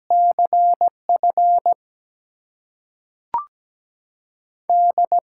Morse code, heard as a keyed tone, about 17 wpm, spaced as for 5 wpm, 700 Hz. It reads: CFD
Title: USA Call Signs - 5 Characters - 6x Character Spacing - 17wpm